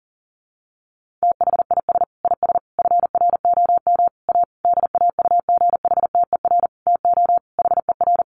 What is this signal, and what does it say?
Morse code, heard as a keyed tone, about 37 wpm, 700 Hz. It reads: THIS IS FROM A DAUGHTER TO HER
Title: Sentences from Top 1500 Words 40wpm